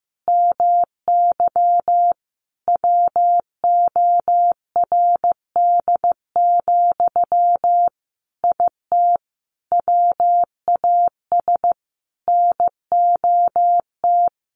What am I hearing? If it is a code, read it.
Morse code, heard as a keyed tone, about 15 wpm, 700 Hz. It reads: MY WORD, IT WAS NOT
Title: Sentences from Top 1100 Words 15wpm